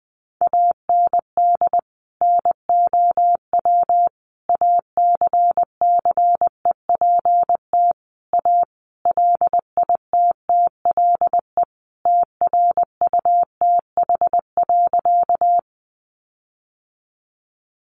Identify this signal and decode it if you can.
Morse code, heard as a keyed tone, about 20 wpm, 700 Hz. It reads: AND NOW ACCEPT A LITTLE TRUTH.
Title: Sentences from Top 1300 Words 20wpm